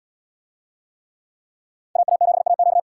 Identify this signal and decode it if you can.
Morse code, heard as a keyed tone, about 69 wpm, 700 Hz. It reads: AA7AQ